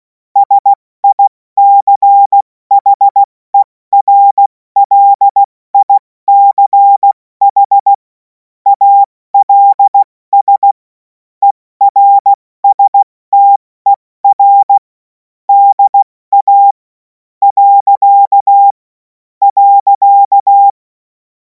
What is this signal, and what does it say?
Morse code, heard as a keyed tone, about 16 wpm, 800 Hz. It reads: SICHERLICH ALS ERSTER DA . .
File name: Morsekurs / Witze / Witz 24. Februar 2026 70 F80